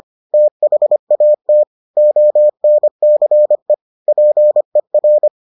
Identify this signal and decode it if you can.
Morse code, heard as a keyed tone, about 25 wpm, 600 Hz. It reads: THAT ONCE PER